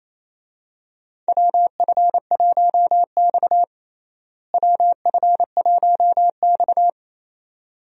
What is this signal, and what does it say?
Morse code, heard as a keyed tone, about 28 wpm, 700 Hz. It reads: WF1X WF1X
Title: USA Call Signs - 4 Characters - Repeated 2x - 28wpm